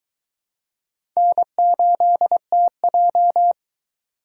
Morse code, heard as a keyed tone, about 22 wpm, 700 Hz. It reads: N8TJ